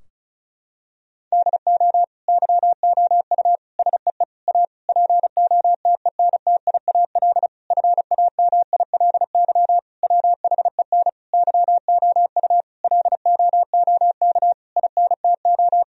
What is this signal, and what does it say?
Morse code, heard as a keyed tone, about 36 wpm, 700 Hz. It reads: DO YOU SEE A POTENTIAL FAMILY WHEN YOU LOOK INTO
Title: Sentences from Top 1000 Words - Review Part 4 - 35wpm